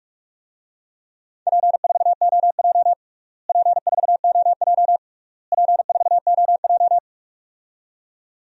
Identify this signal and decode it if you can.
Morse code, heard as a keyed tone, about 46 wpm, 700 Hz. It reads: P4OJ P4OJ P4OJ
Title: DXCC-100 Call Signs - 4 Characters - Repeated 3x - 45wpm